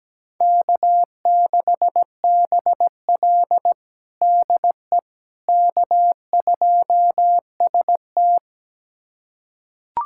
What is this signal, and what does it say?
Morse code, heard as a keyed tone, about 17 wpm, 700 Hz. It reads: K6BL DE K2ST
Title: USA Callsign DE Callsign - 4 Characters - 17wpm